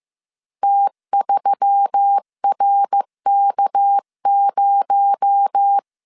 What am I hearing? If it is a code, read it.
Morse code, heard as a keyed tone, about 15 wpm, 800 Hz. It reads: T3RK0